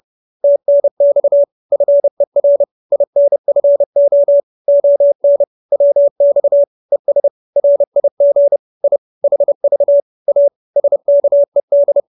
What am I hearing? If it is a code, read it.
Morse code, heard as a keyed tone, about 30 wpm, 570 Hz. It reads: TNX FER INFO ON WX ES RIG I HV A SKED